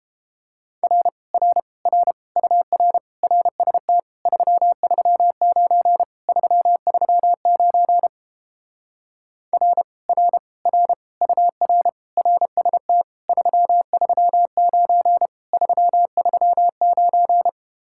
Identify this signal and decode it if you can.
Morse code, heard as a keyed tone, about 31 wpm, 700 Hz. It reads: R R R UR RST 339 339 R R R UR RST 339 339